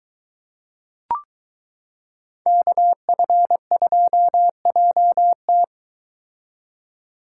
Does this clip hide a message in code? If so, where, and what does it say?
Yes: Morse code from 2.5 to 5.6 s, heard as a keyed tone, about 23 wpm, 700 Hz. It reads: KF2JT